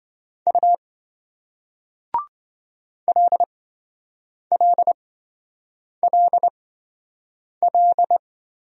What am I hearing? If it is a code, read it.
Morse code, heard as a keyed tone, about 27 wpm, spaced as for 7 wpm, 700 Hz. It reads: U LLLL